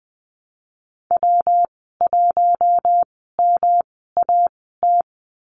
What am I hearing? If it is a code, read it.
Morse code, heard as a keyed tone, about 20 wpm, spaced as for 14 wpm, 700 Hz. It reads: W1MAT
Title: 140 Lesson 23g - Letter f - Call Signs 20wpm